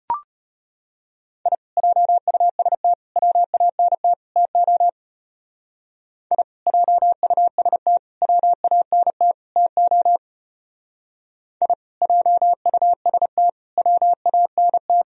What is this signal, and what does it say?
Morse code, heard as a keyed tone, about 29 wpm, 700 Hz. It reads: I JUST WANT TO I JUST WANT TO I JUST WANT
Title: Speed-Racing - Sets of 4 Words 25wpm